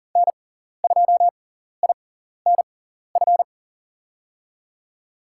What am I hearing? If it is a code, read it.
Morse code, heard as a keyed tone, about 40 wpm, spaced as for 14 wpm, 700 Hz. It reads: N2INF
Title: USA Call Signs - 5 Characters - 6x Character Spacing - 40wpm